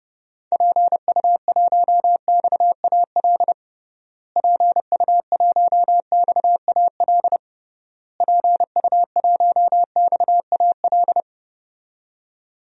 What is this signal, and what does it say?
Morse code, heard as a keyed tone, about 30 wpm, 700 Hz. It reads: PU1XAL PU1XAL PU1XAL
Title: DXCC-100 Call Signs - 6 Characters - Repeated 3x - 30wpm